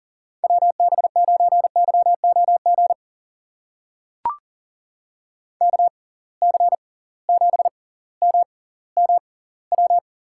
Morse code, heard as a keyed tone, about 40 wpm, spaced as for 21 wpm, 700 Hz. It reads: WB9YOG KC7MMW
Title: USA Call Signs - 6 Characters - 6x Character Spacing - 40wpm